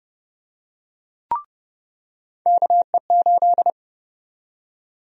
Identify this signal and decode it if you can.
Morse code, heard as a keyed tone, about 30 wpm, 700 Hz. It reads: KE8
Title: USA Call Sign Prefixes - 30wpm